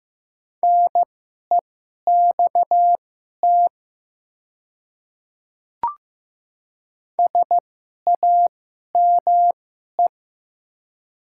Morse code, heard as a keyed tone, about 15 wpm, spaced as for 11 wpm, 700 Hz. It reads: NEXT SAME
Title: 2x Character Spacing - Top 500 Words - 15wpm